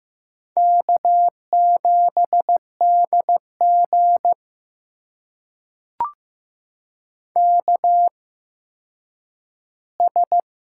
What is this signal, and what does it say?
Morse code, heard as a keyed tone, about 15 wpm, spaced as for 9 wpm, 700 Hz. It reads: K7DG KS
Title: USA Call Signs - 4 Characters - 8x Character Spacing - 15wpm